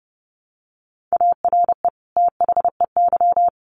Morse code, heard as a keyed tone, about 30 wpm, 700 Hz. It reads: ARE THEY
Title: Sentences from Top 1500 Words 30wpm